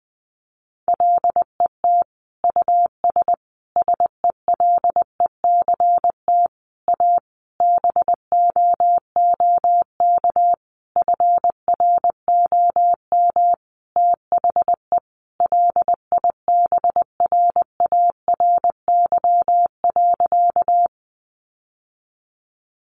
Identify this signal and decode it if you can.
Morse code, heard as a keyed tone, about 20 wpm, 700 Hz. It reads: LET US SELECT A BOOK FROM THE LIBRARY.
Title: Sentences from Top 1300 Words 20wpm